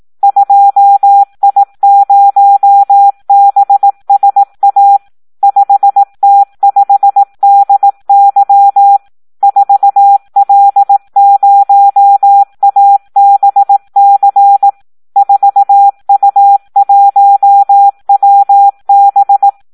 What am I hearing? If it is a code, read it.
Morse code, heard as a keyed tone, about 18 wpm, 800 Hz. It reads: 2I0BSA 5T5DY 4L0ABC 4U1WB
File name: Audio practica CW / SD 1 18